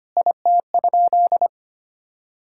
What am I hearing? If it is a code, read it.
Morse code, heard as a keyed tone, about 25 wpm, 700 Hz. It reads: IT?
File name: practice / 25wpm / Sentences with 4-Word Phrases - 25wpm